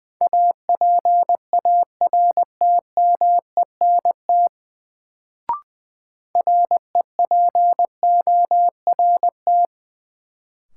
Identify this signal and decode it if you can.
Morse code, heard as a keyed tone, about 21 wpm, 700 Hz. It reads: APARTMENT REPORT
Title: Sentences from Top 900 Words - New Word Drill - 20wpm